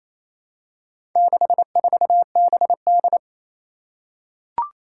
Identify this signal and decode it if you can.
Morse code, heard as a keyed tone, about 28 wpm, 700 Hz. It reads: 64BD
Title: Four Challenging Letter-Number Combinations 28wpm